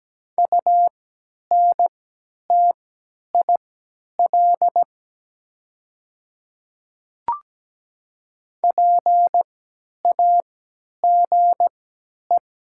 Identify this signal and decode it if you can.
Morse code, heard as a keyed tone, about 17 wpm, spaced as for 10 wpm, 700 Hz. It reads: UNTIL PAGE
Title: 3x Character Spacing - Top 300 Words - 17wpm